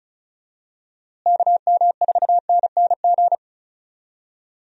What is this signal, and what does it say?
Morse code, heard as a keyed tone, about 34 wpm, 700 Hz. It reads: KM4NNG
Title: USA Call Signs - 6 Characters - Repeated 2x - 35wpm